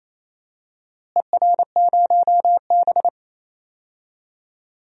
Morse code, heard as a keyed tone, about 28 wpm, 700 Hz. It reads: ER0B
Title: DXCC-100 Call Signs - 4 Characters - Repeated 2x - 28wpm